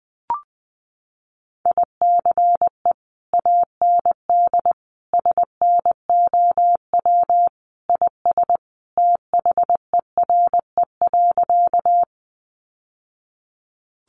Morse code, heard as a keyed tone, about 20 wpm, 700 Hz. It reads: ICE AND SNOW IS THERE.
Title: Sentences from Top 1000 Words 20wpm